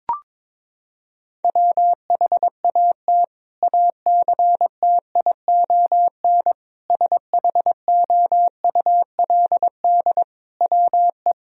Morse code, heard as a keyed tone, about 22 wpm, 700 Hz. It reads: WHAT ACTION SHOULD WE
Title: Sentences from Top 1000 Words - Review Part 4 - 22wpm